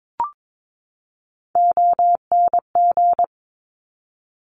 Morse code, heard as a keyed tone, about 22 wpm, 700 Hz. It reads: ONG
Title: Top Three Letter Combinations in Words 22wpm